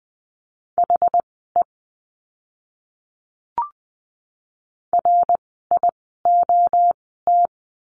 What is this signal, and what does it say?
Morse code, heard as a keyed tone, about 20 wpm, spaced as for 14 wpm, 700 Hz. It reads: HE RIOT
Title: 061 Lesson 11f - Letter h - Words 20wpm